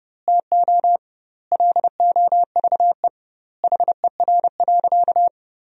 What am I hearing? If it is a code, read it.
Morse code, heard as a keyed tone, about 30 wpm, 700 Hz. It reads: TO LOVE HER.
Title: 2x Word-Spacing - Sentences from Top 400 Words - 30wpm